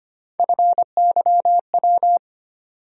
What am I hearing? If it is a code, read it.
Morse code, heard as a keyed tone, about 25 wpm, 700 Hz. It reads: FYW
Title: USA Call Sign Postfixes - 3 Letters 25wpm